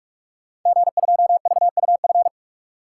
Morse code, heard as a keyed tone, about 44 wpm, 690 Hz. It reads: G2VUF
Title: Code Groups - 5 Letter:Numbers - 45wpm